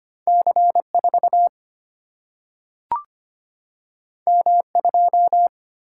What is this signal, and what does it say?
Morse code, heard as a keyed tone, about 25 wpm, 700 Hz. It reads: C 4 M 2
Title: Single-Letter Single-Number 25wpm